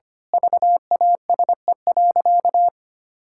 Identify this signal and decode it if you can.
Morse code, heard as a keyed tone, about 25 wpm, 700 Hz. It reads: VASE.
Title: Pangram Challenge v2 - 25wpm